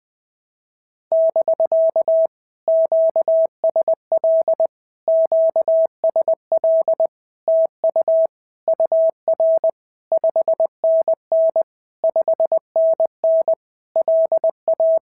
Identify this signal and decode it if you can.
Morse code, heard as a keyed tone, about 20 wpm, 650 Hz. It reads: <BK> QSL QSL TU UR 5NN 5NN LA
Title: POTA QSO Practice - 20wpm